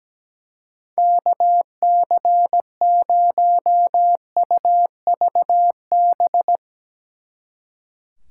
Morse code, heard as a keyed tone, about 17 wpm, 700 Hz. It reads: KC0UVB
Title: Speed Racing - USA 2x3 Call Signs 17wpm